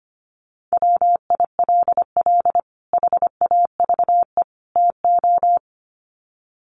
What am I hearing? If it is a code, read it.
Morse code, heard as a keyed tone, about 25 wpm, 700 Hz. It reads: WILL HAVE TO
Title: Sets of 3 Words 25wpm